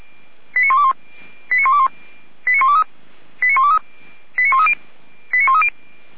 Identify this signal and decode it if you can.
Five-tone selective call (ZVEI), heard as quick runs of five tones, tones of about 70 ms. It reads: 89121 89121 89123 89123 89130 89130